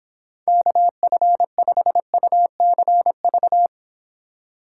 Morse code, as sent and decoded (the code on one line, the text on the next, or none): -.- ..-. ..... ..- -.-. ...-
KF5UCV